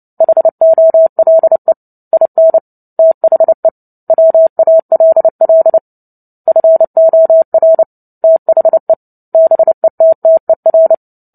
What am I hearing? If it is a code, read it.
Morse code, heard as a keyed tone, about 30 wpm, 650 Hz. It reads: HOLE IN THE WALL FOR THE BETTER
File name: Book Two / 03-Chapter 3 - 29 WPM